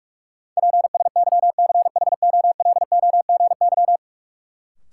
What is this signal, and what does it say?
Morse code, heard as a keyed tone, about 44 wpm, 700 Hz. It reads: PSYCHOLOGY
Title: Sentences from Top 1400 Words - New Word Drill Alt - 45wpm